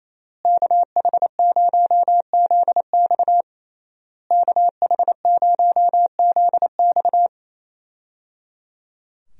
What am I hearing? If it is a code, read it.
Morse code, heard as a keyed tone, about 28 wpm, 700 Hz. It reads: KH0ZX KH0ZX
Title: USA Call Signs - 5 Characters - Repeated 3x - 28wpm